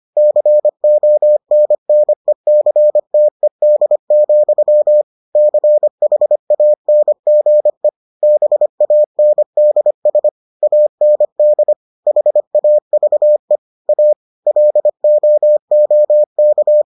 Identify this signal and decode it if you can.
Morse code, heard as a keyed tone, about 25 wpm, 600 Hz. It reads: CONNECTED, CHANGE BANDS AND HAVE A LOOK